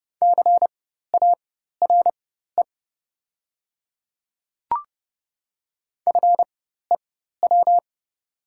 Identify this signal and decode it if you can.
Morse code, heard as a keyed tone, about 30 wpm, spaced as for 14 wpm, 700 Hz. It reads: CARE FEW
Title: 4x Character Spacing - Top 300 Words - 30wpm